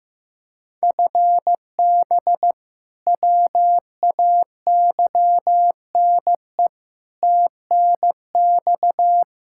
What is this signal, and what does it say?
Morse code, heard as a keyed tone, about 15 wpm, 700 Hz. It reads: FB WAYNE TNX
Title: K1USN SST QSO Practice 15wpm